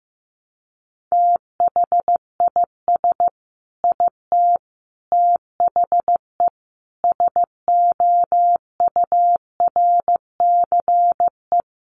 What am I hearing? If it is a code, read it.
Morse code, heard as a keyed tone, about 15 wpm, 700 Hz. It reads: THIS IT THE SOURCE
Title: Sentences from Top 800 Words 15wpm